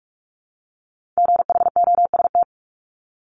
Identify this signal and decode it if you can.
Morse code, heard as a keyed tone, about 46 wpm, 700 Hz. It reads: GHOST
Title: Multi-letter Phonemes Repeated 3x 45wpm